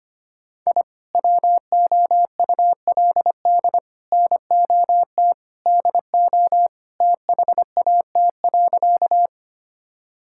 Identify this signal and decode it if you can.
Morse code, heard as a keyed tone, about 25 wpm, 700 Hz. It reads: I WOULD NOT DO THAT.
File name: practice / 25wpm / Sentences with 3-Word Phrases - 25wpm